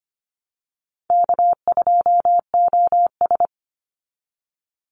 Morse code, heard as a keyed tone, about 25 wpm, 700 Hz. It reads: K2OS
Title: Call Signs Repeated 2x - 25wpm